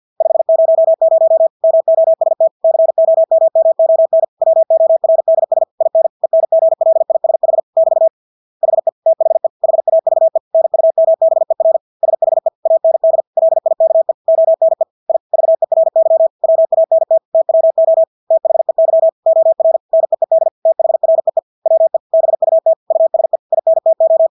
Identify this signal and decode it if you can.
Morse code, heard as a keyed tone, about 50 wpm, 650 Hz. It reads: 500 MOST COMMON WORDS IN ENGLISH <BT> HE THE HAVE NUMBER SHE AND LIKE ONE I VERY WANT TWO THEY OR NEED THREE WE BUT USE INTO
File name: Engelske ord 50 wpm